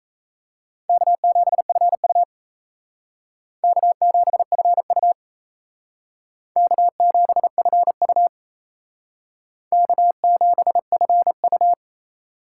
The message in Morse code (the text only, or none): K7FU K7FU K7FU K7FU